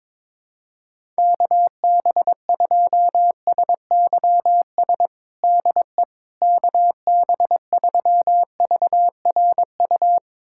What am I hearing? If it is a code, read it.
Morse code, heard as a keyed tone, about 22 wpm, 700 Hz. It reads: KB2SYS DE KB3VRU
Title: USA Callsign DE Callsign - 6 Characters - 22wpm